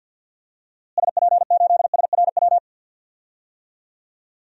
Morse code, heard as a keyed tone, about 51 wpm, 700 Hz. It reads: SP8SRW
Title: DXCC-100 Call Signs - 6 Characters - Repeated 3x - 50wpm